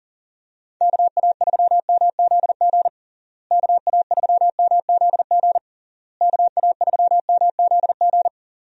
Morse code, heard as a keyed tone, about 37 wpm, 700 Hz. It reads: KA3MZG KA3MZG KA3MZG
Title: USA Call Signs - 6 Characters - Repeated 3x - 40wpm